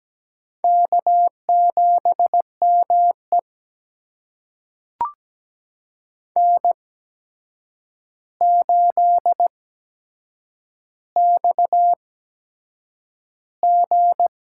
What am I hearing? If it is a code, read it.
Morse code, heard as a keyed tone, about 17 wpm, spaced as for 7 wpm, 700 Hz. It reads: K7ME N8XG
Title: USA Call Signs - 4 Characters - 8x Character Spacing - 17wpm